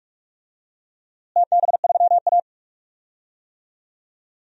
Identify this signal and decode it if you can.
Morse code, heard as a keyed tone, about 45 wpm, 700 Hz. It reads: TB3A
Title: DXCC-100 Call Signs - 4 Characters - Repeated 2x - 45wpm